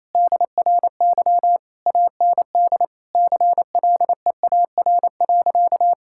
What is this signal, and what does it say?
Morse code, heard as a keyed tone, about 28 wpm, 700 Hz. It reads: DRY AND CLEAR.
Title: Sentences from Top 1600 Words 28wpm